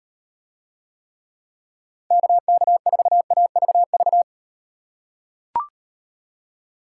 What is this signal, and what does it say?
Morse code, heard as a keyed tone, about 38 wpm, 700 Hz. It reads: KK4AVV